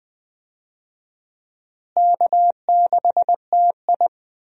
Morse code, heard as a keyed tone, about 20 wpm, 700 Hz. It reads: K6TI
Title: USA Call Signs - 4 Characters - Repeated 2x - 20wpm